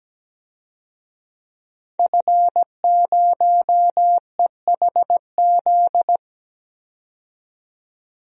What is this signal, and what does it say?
Morse code, heard as a keyed tone, about 17 wpm, 700 Hz. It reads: F0EHZ
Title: DXCC-100 Call Signs - 5 Characters - 17wpm